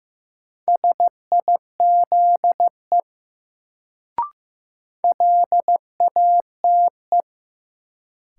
Morse code, heard as a keyed tone, about 15 wpm, 700 Hz. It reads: SIZE LATE
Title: Sets of 3 Words - New Word Drill - 15wpm